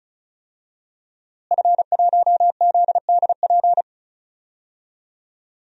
Morse code, heard as a keyed tone, about 36 wpm, 700 Hz. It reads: F1ZDP
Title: DXCC-100 Call Signs - 5 Characters - Repeated 3x - 35wpm